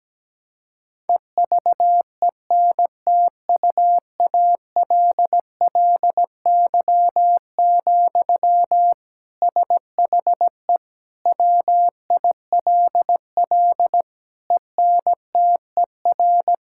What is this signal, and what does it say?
Morse code, heard as a keyed tone, about 17 wpm, 700 Hz. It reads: EVENTUALLY, SHE WILL ENTER
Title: Sentences from Top 1000 Words - Review Part 6 - 17wpm